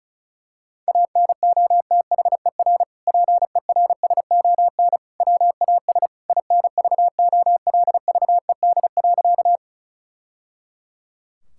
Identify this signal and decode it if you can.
Morse code, heard as a keyed tone, about 34 wpm, 700 Hz. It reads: ANOTHER PERSON WAS INVOLVED.
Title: Sentences from Top 1000 Words - Review Part 5 - 35wpm